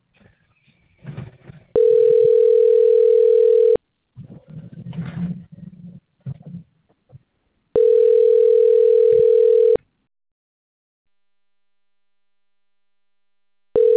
Ringback tone (North America), 440 + 480 Hz.